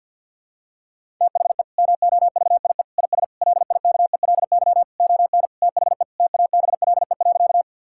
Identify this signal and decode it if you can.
Morse code, heard as a keyed tone, about 52 wpm, 700 Hz. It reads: THE MOVIE IS LIKELY ON THE TABLE.
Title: Sentences from Top 1000 Words - Review Part 5 - 50wpm